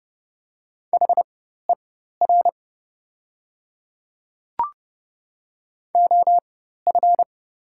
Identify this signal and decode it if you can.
Morse code, heard as a keyed tone, about 30 wpm, spaced as for 14 wpm, 700 Hz. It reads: HER OF
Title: 4x Character Spacing - Top 300 Words - 30wpm